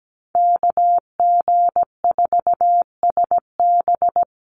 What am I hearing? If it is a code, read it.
Morse code, heard as a keyed tone, about 17 wpm, 700 Hz. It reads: KG4SB
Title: USA Call Signs Repeated 3x 17wpm